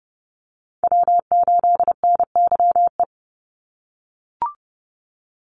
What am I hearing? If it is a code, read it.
Morse code, heard as a keyed tone, about 30 wpm, 700 Hz. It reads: W8NYE
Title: USA Call Signs 30wpm